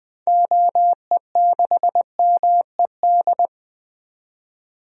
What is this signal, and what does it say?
Morse code, heard as a keyed tone, about 21 wpm, 700 Hz. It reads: OE6MED
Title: DXCC-100 Call Signs - 6 Characters - 20wpm